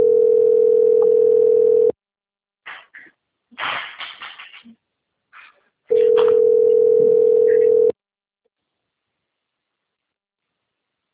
Ringback tone (North America), 440 + 480 Hz.